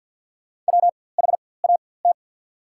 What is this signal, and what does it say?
Morse code, heard as a keyed tone, about 54 wpm, spaced as for 24 wpm, 690 Hz. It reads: WHAT
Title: 4x Character Spacing - Top 100 Words - 50wpm